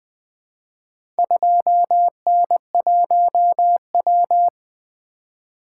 Morse code, heard as a keyed tone, about 20 wpm, 700 Hz. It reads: 2N1W
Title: Four Letter-Number Combinations - 20wpm